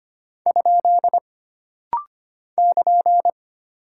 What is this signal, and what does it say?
Morse code, heard as a keyed tone, about 25 wpm, 700 Hz. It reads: ? <KN>
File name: Single Punctuation-Prosign - Rapid-Fire - 25wpm